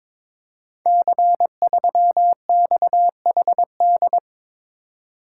Morse code, heard as a keyed tone, about 22 wpm, 700 Hz. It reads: C3XHD
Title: Code Groups - 5 Letter:Numbers - 22wpm